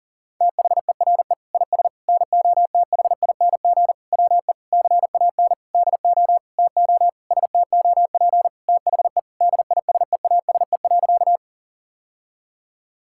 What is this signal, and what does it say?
Morse code, heard as a keyed tone, about 40 wpm, 700 Hz. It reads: THERE IS NOTHING WE CAN DO TO STOP THE DISEASE.
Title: Sentences from Top 1000 Words - Review Part 6 - 40wpm